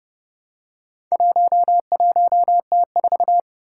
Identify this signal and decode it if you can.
Morse code, heard as a keyed tone, about 30 wpm, 690 Hz. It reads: 11T4
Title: Four Challenging Letter-Number Combinations 30wpm